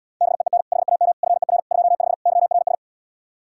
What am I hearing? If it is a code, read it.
Morse code, heard as a keyed tone, about 77 wpm, 700 Hz. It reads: BEEN HAM FER 25 YRS